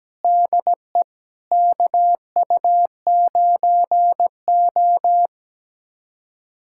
Morse code, heard as a keyed tone, about 17 wpm, 700 Hz. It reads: DE KU9O
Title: USA Callsign DE Callsign 17wpm